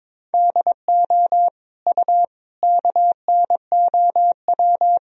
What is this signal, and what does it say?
Morse code, heard as a keyed tone, about 22 wpm, 700 Hz. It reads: DO U KNOW
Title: Sentences from Top 100 Words and Abbreviations 22wpm